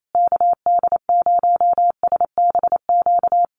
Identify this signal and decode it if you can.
Morse code, heard as a keyed tone, about 28 wpm, 700 Hz. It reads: KD0SBQ
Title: USA Call Signs Repeated 3x 28wpm